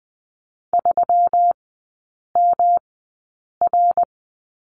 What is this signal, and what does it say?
Morse code, heard as a keyed tone, about 20 wpm, spaced as for 8 wpm, 700 Hz. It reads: 3MR
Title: 152 Lesson 25e - Letter p - Three Characters 20wpm